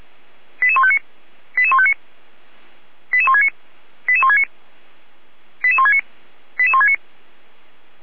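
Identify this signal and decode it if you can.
Five-tone selective call (ZVEI), heard as quick runs of five tones, tones of about 70 ms. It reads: 88159 88159 88169 88169 88169 88169
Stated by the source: ZVEI five-tone selective call 88147 received on SDR.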